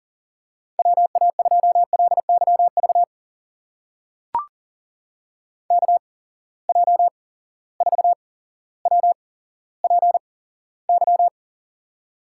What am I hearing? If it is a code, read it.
Morse code, heard as a keyed tone, about 42 wpm, spaced as for 17 wpm, 700 Hz. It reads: WA2LYV KJ4WPY
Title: USA Call Signs - 8x Character Spacing - 40wpm